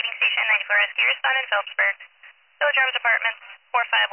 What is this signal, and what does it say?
Emergency Alert System (SAME) tones, heard as harsh buzzing bursts, with speech. It is heard from 0.4 to 3.2 s.